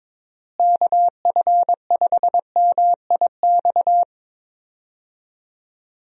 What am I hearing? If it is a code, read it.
Morse code, heard as a keyed tone, about 22 wpm, 700 Hz. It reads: KF5MIX
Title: USA Call Signs - 6 Characters - Repeated 3x - 22wpm